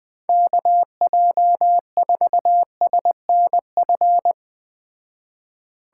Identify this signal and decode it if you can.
Morse code, heard as a keyed tone, about 20 wpm, 700 Hz. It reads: KJ4SNF
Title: Speed Racing - USA 2x3 Call Signs 17wpm